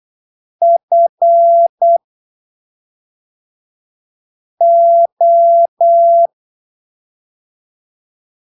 Morse code, heard as a keyed tone, about 8 wpm, spaced as for 3 wpm, 680 Hz. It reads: FO
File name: arquivos / texto / texto 07 08ppm